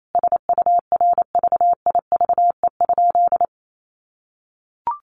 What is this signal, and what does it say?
Morse code, heard as a keyed tone, about 28 wpm, 700 Hz. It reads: SURVIVE?
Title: A Sound Of Thunder 28wpm